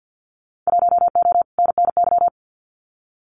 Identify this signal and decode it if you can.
Morse code, heard as a keyed tone, about 50 wpm, 700 Hz. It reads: 1O NNY
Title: Winter Field Day Exchanges Encore 50wpm